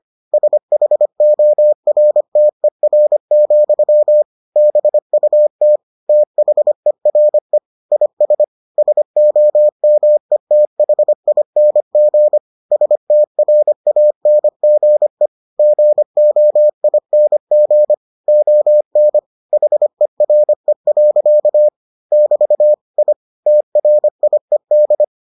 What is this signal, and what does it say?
Morse code, heard as a keyed tone, about 25 wpm, 600 Hz. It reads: SHORTER, BUT THERE IS SOMETHING STRANGE GOING ON HERE. <BT> I TRIED